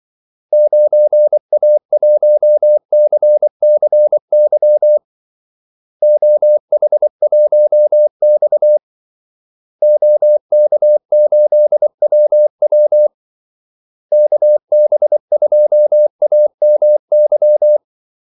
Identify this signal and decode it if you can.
Morse code, heard as a keyed tone, about 24 wpm, 600 Hz. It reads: 9A1CCY OH1X OK8WW KB2AMY
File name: Callsigns-24wpm